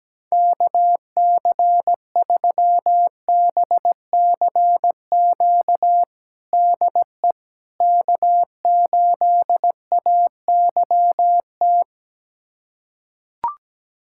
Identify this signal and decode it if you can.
Morse code, heard as a keyed tone, about 17 wpm, 700 Hz. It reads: KC3BCQ DE K8AYT
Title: USA Callsign DE Callsign 17wpm